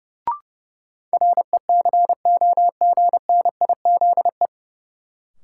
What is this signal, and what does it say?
Morse code, heard as a keyed tone, about 30 wpm, 700 Hz. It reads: RECOGNIZE
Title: Sentences from Top 1400 Words - New Word Drill Alt - 30wpm